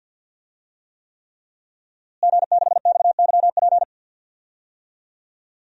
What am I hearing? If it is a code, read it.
Morse code, heard as a keyed tone, about 52 wpm, 700 Hz. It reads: G6XYP
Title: DXCC-100 Call Signs - 5 Characters - Repeated 3x - 50wpm